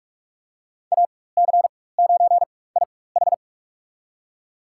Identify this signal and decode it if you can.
Morse code, heard as a keyed tone, about 41 wpm, spaced as for 20 wpm, 700 Hz. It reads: AC9IH
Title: USA Call Signs - 4x Character Spacing - 45wpm